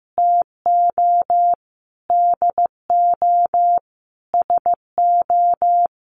Morse code, heard as a keyed tone, about 15 wpm, 700 Hz. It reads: TO DO SO
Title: Sentences from Top 600 Words 15wpm